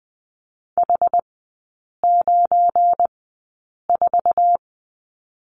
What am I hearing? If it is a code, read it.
Morse code, heard as a keyed tone, about 20 wpm, spaced as for 8 wpm, 700 Hz. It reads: H94
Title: 172 Lesson 28e - Number 9 - Three Characters 20wpm